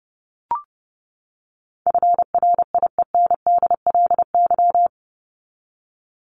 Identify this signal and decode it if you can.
Morse code, heard as a keyed tone, about 30 wpm, 700 Hz. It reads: FRIENDLY
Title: LY Suffix - Top 100 Words with Base Words 30wpm